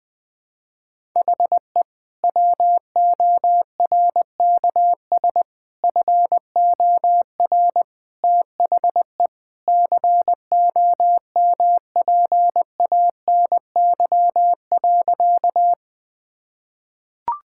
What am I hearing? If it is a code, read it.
Morse code, heard as a keyed tone, about 21 wpm, 700 Hz. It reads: HE WORKS FOR THE COMPANY.
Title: Sentences with 2-Word Phrases - 20wpm